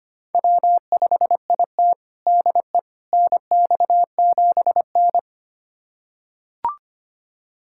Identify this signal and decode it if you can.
Morse code, heard as a keyed tone, about 25 wpm, 700 Hz. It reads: W5IT DE NX7N
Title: USA Callsign DE Callsign - 4 Characters - 25wpm